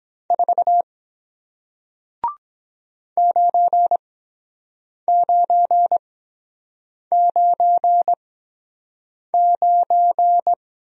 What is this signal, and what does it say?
Morse code, heard as a keyed tone, about 23 wpm, spaced as for 7 wpm, 700 Hz. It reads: 4 9999